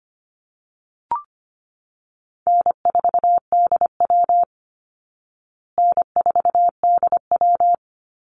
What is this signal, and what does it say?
Morse code, heard as a keyed tone, about 25 wpm, 700 Hz. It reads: N4DW N4DW
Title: Easy Call Signs Repeated 3x - 25wpm